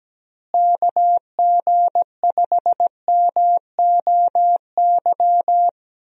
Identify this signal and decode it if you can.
Morse code, heard as a keyed tone, about 17 wpm, 700 Hz. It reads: KG5MOY
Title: USA Call Signs - 6 Characters - Repeated 2x - 17wpm